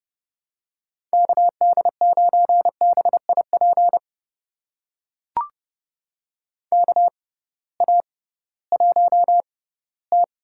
Morse code, heard as a keyed tone, about 30 wpm, spaced as for 18 wpm, 700 Hz. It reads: KD9BIP KA1T
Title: USA Call Signs - 6 Characters - 6x Character Spacing - 30wpm